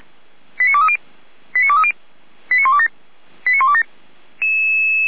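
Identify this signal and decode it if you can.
Five-tone selective call (ZVEI), heard as quick runs of five tones, tones of about 70 ms. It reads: 89230 89230 89127 89127
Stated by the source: ZVEI five-tone selective call received on SDR.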